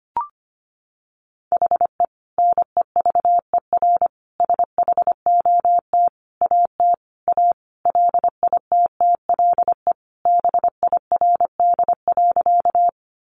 Morse code, heard as a keyed tone, about 25 wpm, 700 Hz. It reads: HE NEVER SHOT AT A LITTLE BIRD.
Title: Sentences from Top 1100 Words 25wpm